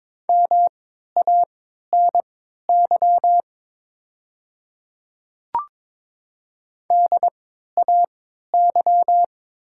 Morse code, heard as a keyed tone, about 22 wpm, spaced as for 12 wpm, 700 Hz. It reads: MANY DAY
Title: 3x Character Spacing - Top 100 Words - 22wpm